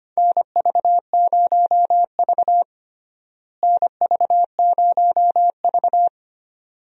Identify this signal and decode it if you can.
Morse code, heard as a keyed tone, about 25 wpm, 700 Hz. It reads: NV0V NV0V
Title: USA Call Signs - 4 Characters - Repeated 2x - 25wpm